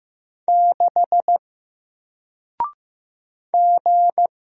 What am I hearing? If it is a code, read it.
Morse code, heard as a keyed tone, about 15 wpm, 700 Hz. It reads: 6 G
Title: Single Letter-Number 15wpm